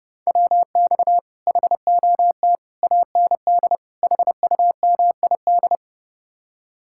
Morse code, heard as a keyed tone, about 37 wpm, 700 Hz. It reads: WX HOT AND HUMID
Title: Speed-Racing Common QSO Phrases Encore - 30wpm